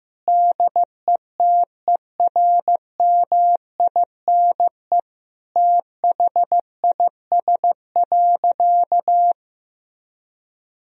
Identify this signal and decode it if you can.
Morse code, heard as a keyed tone, about 15 wpm, 700 Hz. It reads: DETERMINE THIS.
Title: Sentences from Top 1000 Words - Review Part 4 - 15wpm